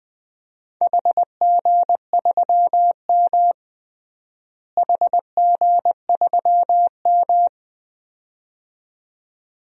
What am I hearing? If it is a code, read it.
Morse code, heard as a keyed tone, about 21 wpm, 700 Hz. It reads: HG3M HG3M
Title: DXCC-100 Call Signs - 4 Characters - Repeated 2x - 20wpm